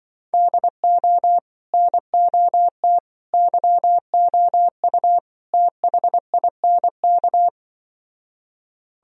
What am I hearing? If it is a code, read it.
Morse code, heard as a keyed tone, about 25 wpm, 700 Hz. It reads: DO NOT YOU THINK
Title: Speed-Racing - Sets of 4 Words 20wpm